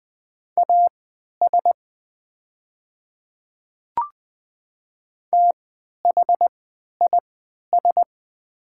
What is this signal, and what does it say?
Morse code, heard as a keyed tone, about 20 wpm, spaced as for 11 wpm, 700 Hz. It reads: AS THIS
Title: 3x Character Spacing - Top 100 Words - 20wpm